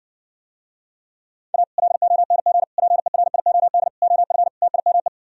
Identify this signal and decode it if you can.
Morse code, heard as a keyed tone, about 59 wpm, 700 Hz. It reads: A LONG PERIOD OF TIME